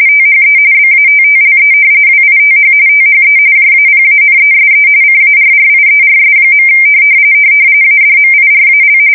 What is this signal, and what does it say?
RTTY, about 45.45 baud. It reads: THE BATTLE WITH THE JAPANESE CARRIERS. ADMIRAL
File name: fsk170 / battleshipnj